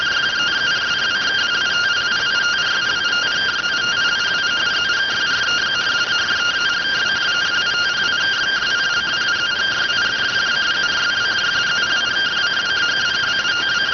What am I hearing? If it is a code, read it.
Two-tone teleprinter signal, about 170 Hz shift.